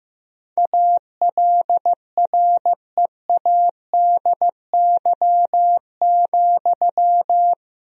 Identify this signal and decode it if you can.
Morse code, heard as a keyed tone, about 15 wpm, 700 Hz. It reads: ALREADY,
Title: Sentences from Top 1000 Words - Review Part 4 - 15wpm